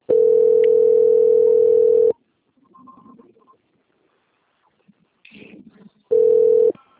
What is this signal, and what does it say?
Ringback tone (North America), 440 + 480 Hz.
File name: out- - - - - .473